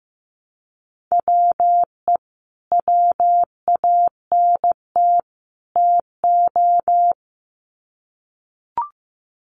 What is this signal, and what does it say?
Morse code, heard as a keyed tone, about 15 wpm, 700 Hz. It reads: WE WANT TO
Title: Sets of 3 Words 15wpm